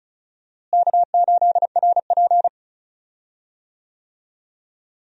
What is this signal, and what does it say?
Morse code, heard as a keyed tone, about 34 wpm, 700 Hz. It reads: K8RP
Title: USA Call Signs - 4 Characters - Repeated 3x - 35wpm